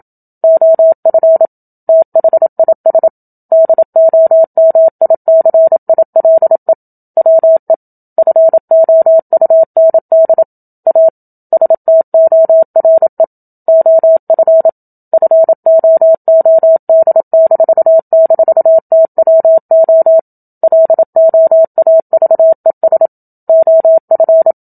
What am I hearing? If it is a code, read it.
Morse code, heard as a keyed tone, about 27 wpm, 650 Hz. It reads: OF THIS DOMICILE WE FOUND A STORE OF FOOD--TWO LOAVES OF
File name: Book Two / 01-Chapter 1 - 27 WPM